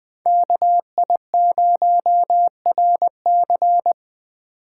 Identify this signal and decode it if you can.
Morse code, heard as a keyed tone, about 20 wpm, 700 Hz. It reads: KI0RC